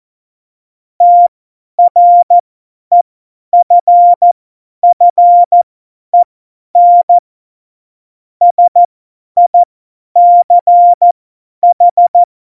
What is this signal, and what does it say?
Morse code, heard as a keyed tone, about 14 wpm, spaced as for 10 wpm, 700 Hz. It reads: TREFFEN SICH
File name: Morsekurs / Witze / Witz 12. Juni 2018, 50 F70